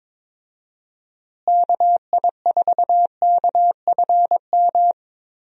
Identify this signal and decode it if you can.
Morse code, heard as a keyed tone, about 22 wpm, 700 Hz. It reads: KI4KFM
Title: USA Call Signs - 6 Characters - Repeated 3x - 22wpm